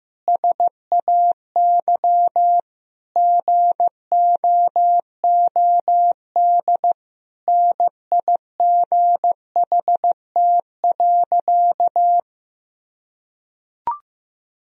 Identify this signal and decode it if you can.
Morse code, heard as a keyed tone, about 15 wpm, 700 Hz. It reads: SAY GOOD NIGHT.
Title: Sentences from Top 1000 Words - Review Part 4 - 15wpm